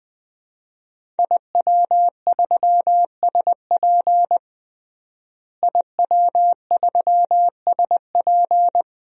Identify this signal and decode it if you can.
Morse code, heard as a keyed tone, about 20 wpm, 700 Hz. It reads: IW3SP IW3SP
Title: DXCC-100 Call Signs - 5 Characters - Repeated 2x - 20wpm